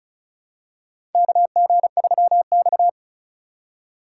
Morse code, heard as a keyed tone, about 36 wpm, 700 Hz. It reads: KG3X
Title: USA Call Signs - 4 Characters - 4x Character Spacing - 35wpm